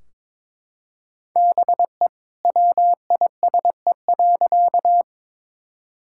Morse code, heard as a keyed tone, about 21 wpm, 700 Hz. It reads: BE WISE.
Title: Sentences from Top 1600 Words 22wpm